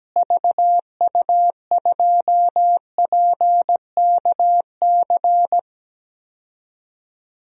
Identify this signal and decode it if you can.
Morse code, heard as a keyed tone, about 17 wpm, 700 Hz. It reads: VU2PKC
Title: DXCC-100 Call Signs - 6 Characters - Repeated 3x - 17wpm